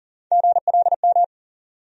Morse code, heard as a keyed tone, about 40 wpm, 700 Hz. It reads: GLM
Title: USA Call Sign Postfixes - 3 Letters 40wpm